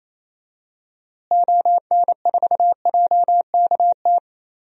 Morse code, heard as a keyed tone, about 28 wpm, 700 Hz. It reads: ON4JKT
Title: DXCC-100 Call Signs - 6 Characters - Repeated 2x - 28wpm